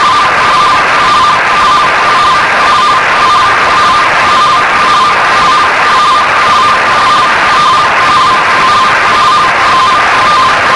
Two-tone teleprinter signal, about 170 Hz shift.
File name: ao-10 RTTY 17 Feb 1985 09h15 oz6bl